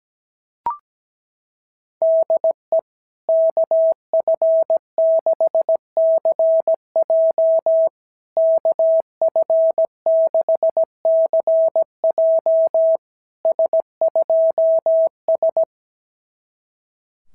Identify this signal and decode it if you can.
Morse code, heard as a keyed tone, about 18 wpm, 650 Hz. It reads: DE KF6CJ KF6CJ S2S
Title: SOTA QSO Practice 17wpm